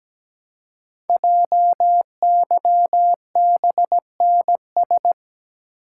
Morse code, heard as a keyed tone, about 17 wpm, 700 Hz. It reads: JYBNS
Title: Code Groups - 5 Letters - 17wpm